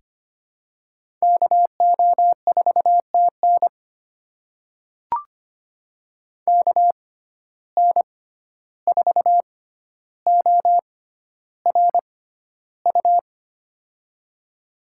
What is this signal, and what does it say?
Morse code, heard as a keyed tone, about 25 wpm, spaced as for 12 wpm, 700 Hz. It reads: KO4TN KN4ORU
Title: USA Call Signs - 6x Character Spacing - 25wpm